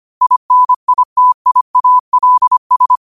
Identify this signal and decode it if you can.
Morse code, heard as a keyed tone, about 25 wpm, 1000 Hz. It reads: INITIALS
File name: qotdfiles / 25wpm